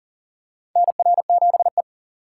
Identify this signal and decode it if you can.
Morse code, heard as a keyed tone, about 40 wpm, 700 Hz. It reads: NR7E
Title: USA Call Signs - 4 Characters - 3x Character Spacing - 40wpm